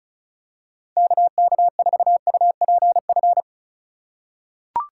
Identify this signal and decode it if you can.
Morse code, heard as a keyed tone, about 34 wpm, 700 Hz. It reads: KK4UPF
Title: USA Call Signs - 2x Character Spacing - 35wpm